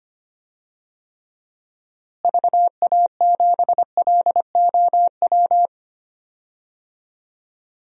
Morse code, heard as a keyed tone, about 25 wpm, 700 Hz. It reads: VA7LOW
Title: DXCC-100 Call Signs - 6 Characters - Repeated 3x - 25wpm